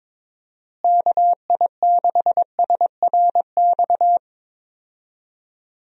Morse code, heard as a keyed tone, about 22 wpm, 700 Hz. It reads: KI6SRX
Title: USA Call Signs - 6 Characters - Repeated 2x - 22wpm